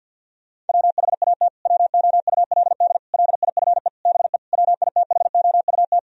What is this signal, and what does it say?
Morse code, heard as a keyed tone, about 46 wpm, 700 Hz. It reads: WHAT WOULD LIFE BE WITHOUT
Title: Sentences from Top 1000 Words - Review Part 3 - 50wpm